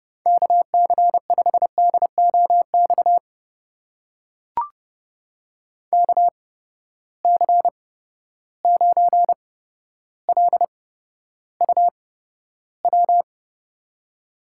Morse code, heard as a keyed tone, about 30 wpm, spaced as for 13 wpm, 700 Hz. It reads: KC5DOX KC9LUW
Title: USA Call Signs - 6 Characters - 8x Character Spacing - 30wpm